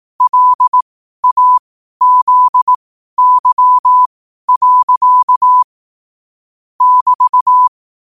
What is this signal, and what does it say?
Morse code, heard as a keyed tone, about 18 wpm, spaced as for 12 wpm, 1000 Hz. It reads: LAZY. <BT>